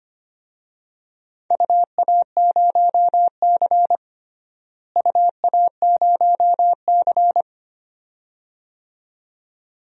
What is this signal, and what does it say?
Morse code, heard as a keyed tone, about 25 wpm, 700 Hz. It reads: UA0C UA0C
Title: DXCC-100 Call Signs - 4 Characters - Repeated 2x - 25wpm